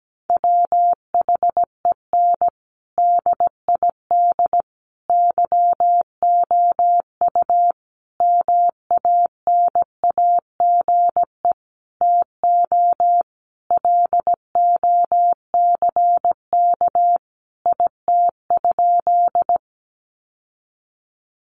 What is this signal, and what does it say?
Morse code, heard as a keyed tone, about 17 wpm, 700 Hz. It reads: WHEN DID YOU MANAGE TO LOCK IT?
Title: Sentences from Top 1400 Words 17wpm